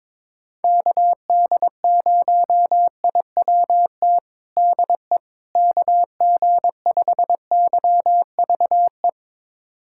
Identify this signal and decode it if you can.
Morse code, heard as a keyed tone, about 21 wpm, 700 Hz. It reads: KD0IWT DE KG5YVE
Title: USA Callsign DE Callsign - 6 Characters - 22wpm